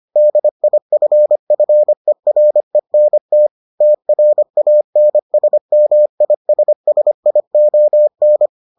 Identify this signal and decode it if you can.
Morse code, heard as a keyed tone, about 25 wpm, 600 Hz. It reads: DIFFERENT TRANSMISSION